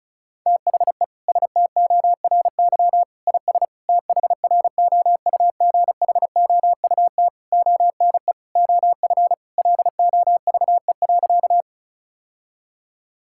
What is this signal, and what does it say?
Morse code, heard as a keyed tone, about 34 wpm, 690 Hz. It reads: THE STORY IS THROUGHOUT ONE OF LOVE.
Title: Sentences from Top 1000 Words - Review Part 5 - 35wpm